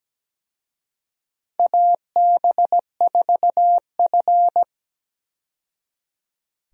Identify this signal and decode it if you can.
Morse code, heard as a keyed tone, about 17 wpm, 700 Hz. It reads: AB4F
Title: USA Call Signs - 4 Characters - 17wpm